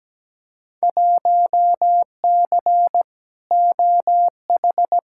Morse code, heard as a keyed tone, about 17 wpm, 700 Hz. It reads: 1C OH
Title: Field Day Exchanges - 17wpm